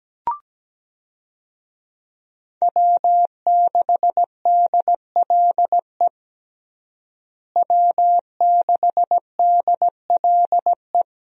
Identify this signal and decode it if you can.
Morse code, heard as a keyed tone, about 17 wpm, 700 Hz. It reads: W6DLE W6DLE
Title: USA Call Signs - 5 Characters - Repeated 3x - 17wpm